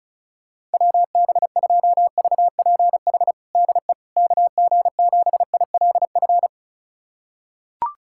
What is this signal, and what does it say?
Morse code, heard as a keyed tone, about 36 wpm, 700 Hz. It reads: WB2VPH DE KG7ILF
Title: USA Callsign DE Callsign 35wpm